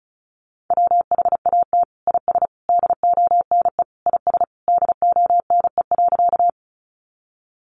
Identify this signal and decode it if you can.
Morse code, heard as a keyed tone, about 34 wpm, 700 Hz. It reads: WHAT IS DONE IS DONE.
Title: Sentences from Top 500 Words 35wpm